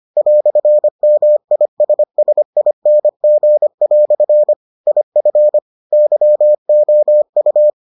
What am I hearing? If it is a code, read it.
Morse code, heard as a keyed tone, about 25 wpm, 600 Hz. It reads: "MISSING" IF YOU